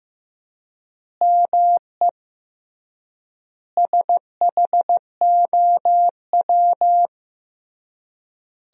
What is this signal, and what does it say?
Morse code, heard as a keyed tone, about 15 wpm, 700 Hz. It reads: ME SHOW
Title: 3x Word-Spacing - Sentences from Top 500 Words - 15wpm